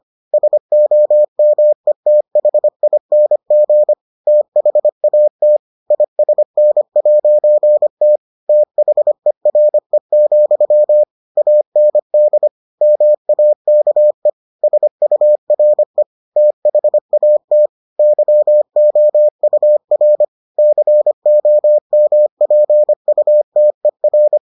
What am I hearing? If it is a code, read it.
Morse code, heard as a keyed tone, about 25 wpm, 600 Hz. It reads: SOMETHING THAT ISN'T THERE, AND MAKE SURE THAT YOUR COMPUTER